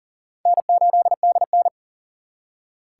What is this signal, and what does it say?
Morse code, heard as a keyed tone, about 50 wpm, 700 Hz. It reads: N8DN